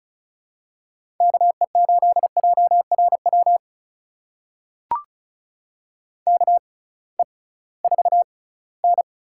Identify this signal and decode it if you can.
Morse code, heard as a keyed tone, about 36 wpm, spaced as for 21 wpm, 700 Hz. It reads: KE8JRW KE4N
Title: USA Call Signs - 6 Characters - 6x Character Spacing - 35wpm